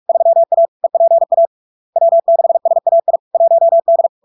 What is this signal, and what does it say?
Morse code, heard as a keyed tone, about 46 wpm, 670 Hz. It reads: 3A EPA W6SAI 1D
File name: LARK-FD-2022-Log-in-CW-45wpm-670-Hz